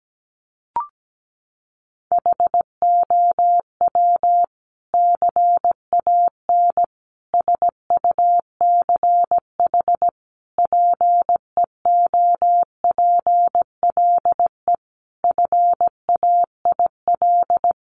Morse code, heard as a keyed tone, about 17 wpm, 700 Hz. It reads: HOW CAN SUCH PEOPLE FAIL
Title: Sentences from Top 1100 Words 17wpm